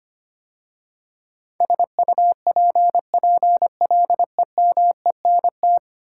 Morse code, heard as a keyed tone, about 25 wpm, 690 Hz. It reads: SUPPLEMENT
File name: 10 Letter Words - First 3 Letters Given - 25wpm